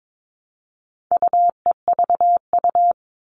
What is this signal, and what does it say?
Morse code, heard as a keyed tone, about 22 wpm, 700 Hz. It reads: UEVU
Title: All Cut Numbers - 4-Digit 22wpm